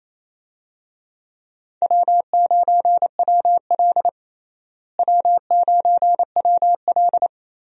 Morse code, heard as a keyed tone, about 28 wpm, 700 Hz. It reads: W9WL W9WL
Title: USA Call Signs - 4 Characters - Repeated 3x - 28wpm